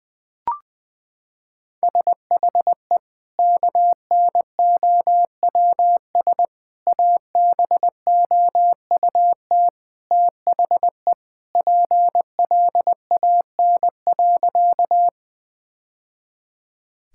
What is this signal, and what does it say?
Morse code, heard as a keyed tone, about 21 wpm, 700 Hz. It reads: SHE KNOWS ABOUT THE PLAN.
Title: Sentences with 2-Word Phrases - 20wpm